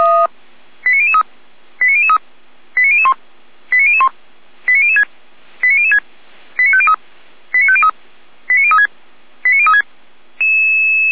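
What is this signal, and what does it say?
Five-tone selective call (ZVEI), heard as quick runs of five tones, tones of about 70 ms. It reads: 89003 89003 89001 89001 89006 89006 89503 89503 89036 89036
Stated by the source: ZVEI five-tone selective call received on SDR, from feuerwehr-orlamuende.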